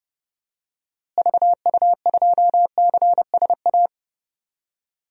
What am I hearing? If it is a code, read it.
Morse code, heard as a keyed tone, about 30 wpm, 700 Hz. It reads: VU2CSA
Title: DXCC-100 Call Signs - 6 Characters - Repeated 2x - 30wpm